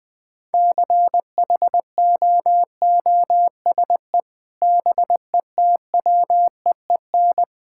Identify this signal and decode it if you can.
Morse code, heard as a keyed tone, about 20 wpm, 700 Hz. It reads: CHOOSE BETWEEN
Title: Sentences with 2-Word Phrases - Spoken Beginning - 20wpm